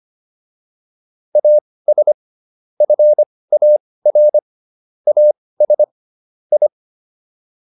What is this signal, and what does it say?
Morse code, heard as a keyed tone, about 25 wpm, spaced as for 18 wpm, 600 Hz. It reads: AS FAR AS I